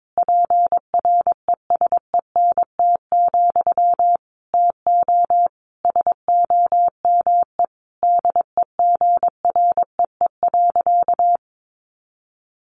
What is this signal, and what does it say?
Morse code, heard as a keyed tone, about 22 wpm, 700 Hz. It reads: PRESENT, TO SOME DEGREE.
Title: Sentences from Top 1500 Words 22wpm